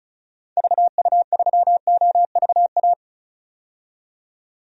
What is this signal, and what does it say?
Morse code, heard as a keyed tone, about 34 wpm, 700 Hz. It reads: VU3OVA
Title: DXCC-100 Call Signs - 6 Characters - 35wpm